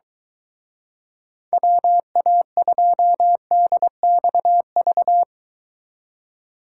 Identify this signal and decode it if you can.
Morse code, heard as a keyed tone, about 24 wpm, 700 Hz. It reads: WA2DXV